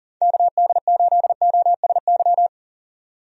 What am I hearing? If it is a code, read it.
Morse code, heard as a keyed tone, about 40 wpm, 700 Hz. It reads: KD8OSY
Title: USA Call Signs - 6 Characters - 40wpm